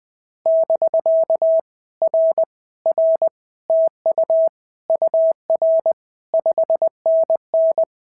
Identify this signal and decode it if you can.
Morse code, heard as a keyed tone, about 20 wpm, 650 Hz. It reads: <BK> R R TU UR 5NN
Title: POTA QSO Practice - 20wpm